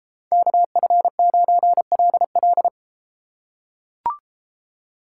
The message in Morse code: KF9LL

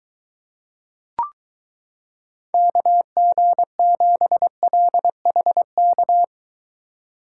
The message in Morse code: KG7LHK

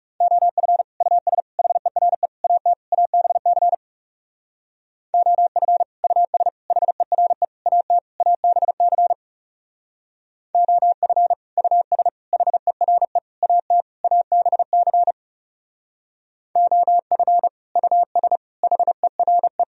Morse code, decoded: OF US HERE AT ABC OF US HERE AT ABC OF US HERE AT ABC OF US HERE

